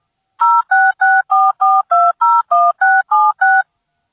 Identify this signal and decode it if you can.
DTMF, heard as telephone touch tones, tones of about 220 ms. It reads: 06644201676